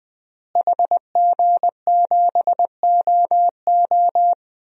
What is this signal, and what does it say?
Morse code, heard as a keyed tone, about 21 wpm, 700 Hz. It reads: HG7OO
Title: DXCC-100 Call Signs - 5 Characters - Repeated 3x - 20wpm